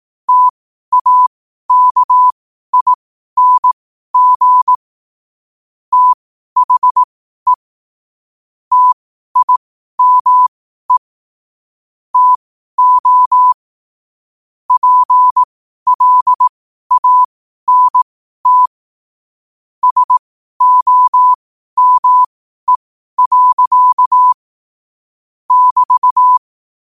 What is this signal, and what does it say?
Morse code, heard as a keyed tone, about 18 wpm, spaced as for 12 wpm, 1000 Hz. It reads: TAKING THE TIME TO PLANT SOME. <BT>
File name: qotdfiles / 12wpm